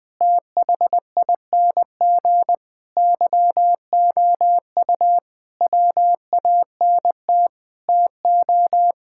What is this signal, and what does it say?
Morse code, heard as a keyed tone, about 20 wpm, 700 Hz. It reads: THING YOU WANT TO
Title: Sets of 8 Words - 20wpm